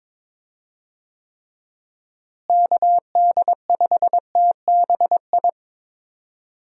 Morse code, heard as a keyed tone, about 22 wpm, 700 Hz. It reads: KD5TBI
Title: USA Call Signs - 6 Characters - Repeated 2x - 22wpm